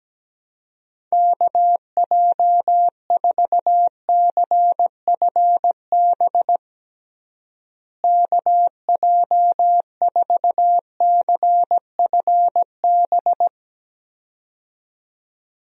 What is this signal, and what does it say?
Morse code, heard as a keyed tone, about 17 wpm, 700 Hz. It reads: KJ4CFB KJ4CFB
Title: USA Call Signs - 6 Characters - Repeated 2x - 17wpm